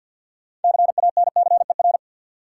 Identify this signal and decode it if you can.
Morse code, heard as a keyed tone, about 51 wpm, 700 Hz. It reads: CANCER
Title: Six Letter Words - 50wpm